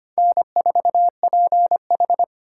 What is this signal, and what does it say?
Morse code, heard as a keyed tone, about 25 wpm, 700 Hz. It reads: N4PH
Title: USA Call Signs - 4 Characters - 2x Character Spacing - 25wpm